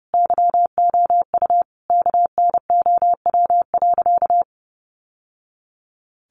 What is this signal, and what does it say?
Morse code, heard as a keyed tone, about 30 wpm, 700 Hz. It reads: YOU KNOW.
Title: Sentences from Top 700 Words 30wpm